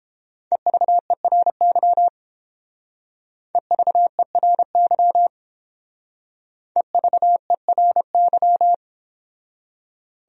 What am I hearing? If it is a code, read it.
Morse code, heard as a keyed tone, about 28 wpm, 700 Hz. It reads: EVERY EVERY EVERY